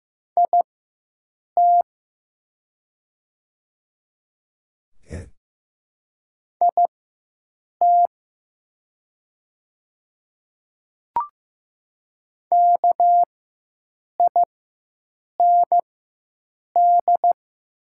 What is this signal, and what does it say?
Morse code, heard as a keyed tone, about 15 wpm, spaced as for 7 wpm, 700 Hz. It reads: IT IT KIND
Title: 4x Character Spacing - Top 200 Words - 15wpm